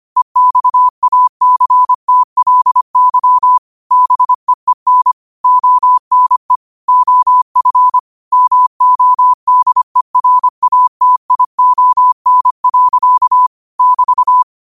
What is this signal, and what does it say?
Morse code, heard as a keyed tone, about 25 wpm, 1000 Hz. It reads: EXACTLY BEEN ONE OF MODERATION. <BT>